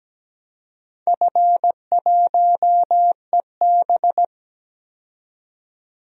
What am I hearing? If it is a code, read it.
Morse code, heard as a keyed tone, about 17 wpm, 700 Hz. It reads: F1EB
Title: DXCC-100 Call Signs - 4 Characters - 17wpm